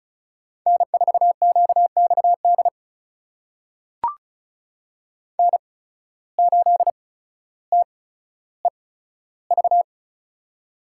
Morse code, heard as a keyed tone, about 36 wpm, spaced as for 15 wpm, 700 Hz. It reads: N4QXD N8TEV
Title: USA Call Signs - 5 Characters - 8x Character Spacing - 35wpm